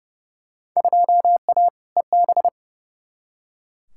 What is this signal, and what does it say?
Morse code, heard as a keyed tone, about 30 wpm, 700 Hz. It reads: 2A EB
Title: Field Day Exchanges - 30wpm